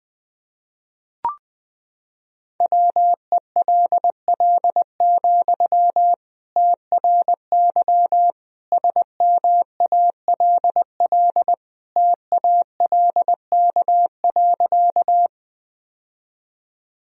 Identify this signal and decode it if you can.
Morse code, heard as a keyed tone, about 21 wpm, 700 Hz. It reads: WELL, TRY SMALL TALK.
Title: Sentences from Top 1000 Words - Review Part 1 - 20wpm